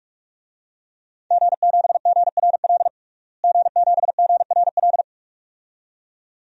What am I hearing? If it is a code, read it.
Morse code, heard as a keyed tone, about 42 wpm, 700 Hz. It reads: G7GRL G7GRL
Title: DXCC-100 Call Signs - 5 Characters - Repeated 2x - 45wpm